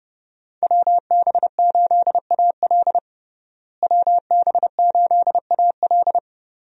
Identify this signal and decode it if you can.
Morse code, heard as a keyed tone, about 30 wpm, 700 Hz. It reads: WB8AL WB8AL
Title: USA Call Signs - 5 Characters - Repeated 2x - 30wpm